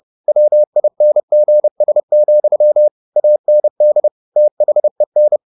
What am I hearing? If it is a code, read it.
Morse code, heard as a keyed tone, about 30 wpm, 600 Hz. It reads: WINGS, AND THEN
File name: the old man and the sea / the old man and the sea 1 30wpm